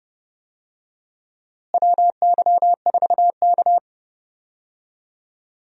Morse code, heard as a keyed tone, about 30 wpm, 700 Hz. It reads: WY4K
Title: USA Call Signs - 4 Characters - Repeated 2x - 30wpm